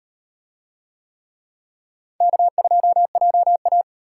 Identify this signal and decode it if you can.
Morse code, heard as a keyed tone, about 38 wpm, 700 Hz. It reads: K2JA